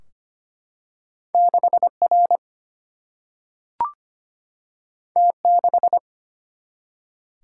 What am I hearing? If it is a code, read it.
Morse code, heard as a keyed tone, about 25 wpm, 700 Hz. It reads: 6R T6
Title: Single-Letter Single-Number 25wpm